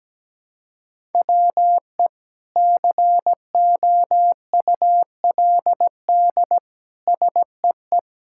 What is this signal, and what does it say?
Morse code, heard as a keyed tone, about 17 wpm, 700 Hz. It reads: WE COULD SEE
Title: Sentences from Top 1000 Words - Review Part 5 - 17wpm